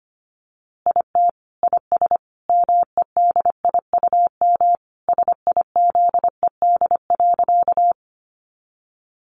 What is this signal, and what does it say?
Morse code, heard as a keyed tone, about 25 wpm, 700 Hz. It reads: IT IS MEDIUM SIZED.
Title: Easy Phrases 25wpm